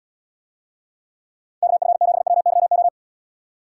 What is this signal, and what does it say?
Morse code, heard as a keyed tone, about 78 wpm, 700 Hz. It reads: KF7WYO